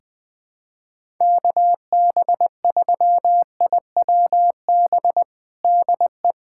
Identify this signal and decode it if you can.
Morse code, heard as a keyed tone, about 21 wpm, 700 Hz. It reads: KB3IWB DE
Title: USA Callsign DE Callsign - 6 Characters - 20wpm